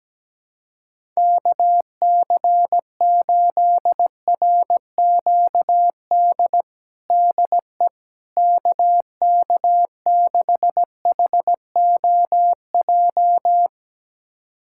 Morse code, heard as a keyed tone, about 17 wpm, 700 Hz. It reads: KC8RQD DE KK6HOJ
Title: USA Callsign DE Callsign - 6 Characters - 17wpm